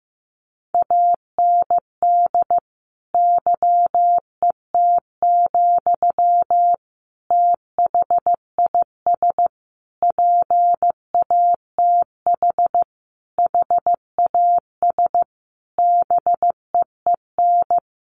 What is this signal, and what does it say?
Morse code, heard as a keyed tone, about 15 wpm, 700 Hz. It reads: AND YET, THIS PATH HAS BEEN
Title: Sentences from Top 1400 Words 15wpm